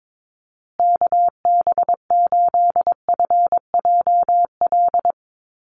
Morse code, heard as a keyed tone, about 22 wpm, 700 Hz. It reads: KB8FJL
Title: USA Call Signs Repeated 3x 22wpm